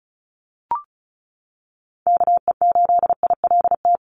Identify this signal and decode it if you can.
Morse code, heard as a keyed tone, about 36 wpm, 700 Hz. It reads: KE8ILT
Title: USA Call Signs 35wpm